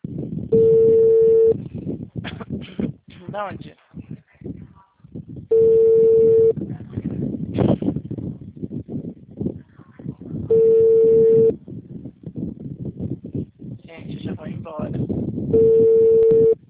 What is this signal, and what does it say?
Ringback tone (China, 450 Hz), with speech.